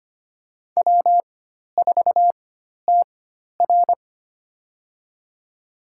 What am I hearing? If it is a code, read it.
Morse code, heard as a keyed tone, about 25 wpm, spaced as for 12 wpm, 700 Hz. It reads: W4TR
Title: USA Call Signs - 4 Characters - 4x Character Spacing - 25wpm